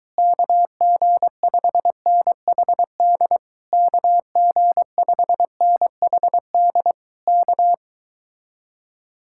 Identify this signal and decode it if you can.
Morse code, heard as a keyed tone, about 22 wpm, 700 Hz. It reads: KG5NHD KG5NHD K